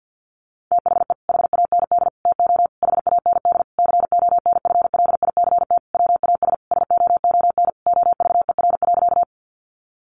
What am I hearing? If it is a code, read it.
Morse code, heard as a keyed tone, about 48 wpm, 700 Hz. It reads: THE HAND TO HAND CONFLICT WAS SOON OVER.
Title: Sentences from Top 1300 Words 50wpm